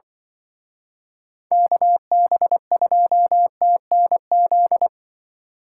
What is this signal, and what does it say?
Morse code, heard as a keyed tone, about 23 wpm, 700 Hz. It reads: KB2TNZ
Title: Speed Racing - USA 2x3 Call Signs 20wpm